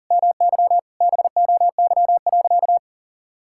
Morse code, heard as a keyed tone, about 40 wpm, 700 Hz. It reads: MY BOY.
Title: Sentences from Top 1000 Words - Review Part 5 - 40wpm